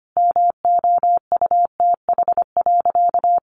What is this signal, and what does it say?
Morse code, heard as a keyed tone, about 25 wpm, 700 Hz. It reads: MOUTH.
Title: Sentences from Top 1500 Words 25wpm